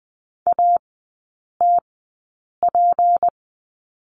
Morse code, heard as a keyed tone, about 20 wpm, spaced as for 8 wpm, 700 Hz. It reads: ATP
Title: 184 Lesson 30e - Letter b - Three Characters 20wpm